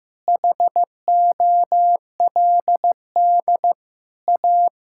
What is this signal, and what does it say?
Morse code, heard as a keyed tone, about 15 wpm, 700 Hz. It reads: HOLD A
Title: Sentences from Top 1600 Words 15wpm